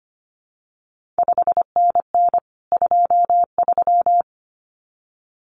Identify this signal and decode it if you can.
Morse code, heard as a keyed tone, about 25 wpm, 700 Hz. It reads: 5NN 23
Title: CQ WW DX Contest Exchange Encore - 25wpm